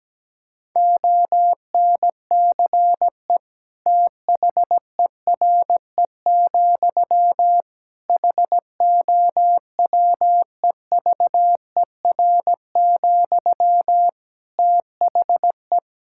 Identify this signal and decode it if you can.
Morse code, heard as a keyed tone, about 17 wpm, 700 Hz. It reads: ONCE THERE, HOWEVER, THE
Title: Sentences from Top 1000 Words - Review Part 6 - 17wpm